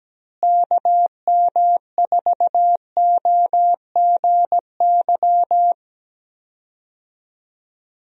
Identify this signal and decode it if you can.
Morse code, heard as a keyed tone, about 17 wpm, 700 Hz. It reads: KM4OGY
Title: USA Call Signs - 6 Characters - Repeated 3x - 17wpm